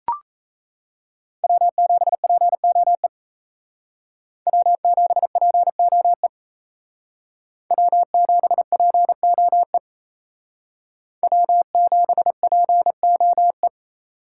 Morse code, heard as a keyed tone, about 35 wpm, 700 Hz. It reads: W7POE W7POE W7POE W7POE